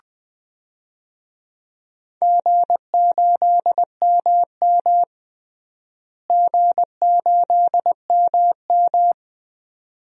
Morse code, heard as a keyed tone, about 20 wpm, 700 Hz. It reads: G8MM G8MM
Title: DXCC-100 Call Signs - 4 Characters - Repeated 2x - 20wpm